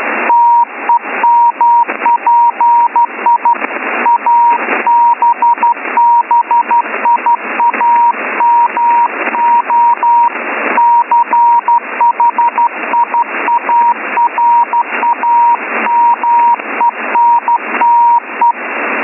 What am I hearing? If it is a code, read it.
Morse code, heard as a keyed tone, about 13 wpm, 950 Hz, with radio noise. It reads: TEMPI ABBIAMO CHIARAMENTE